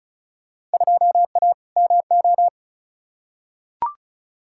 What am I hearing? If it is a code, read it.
Morse code, heard as a keyed tone, about 36 wpm, 700 Hz. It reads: 2A MO
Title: Field Day Exchanges Encore - 35wpm